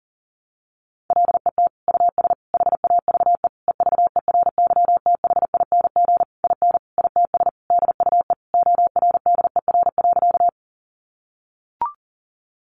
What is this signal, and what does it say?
Morse code, heard as a keyed tone, about 40 wpm, 700 Hz. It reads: LET US HAVE EVERYTHING IN ITS DUE ORDER.
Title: Sentences from Top 900 Words 40wpm